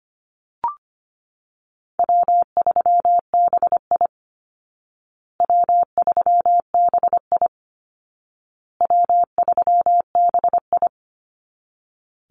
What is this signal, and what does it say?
Morse code, heard as a keyed tone, about 25 wpm, 700 Hz. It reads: W3BI W3BI W3BI
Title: Easy Call Signs Repeated 3x - 25wpm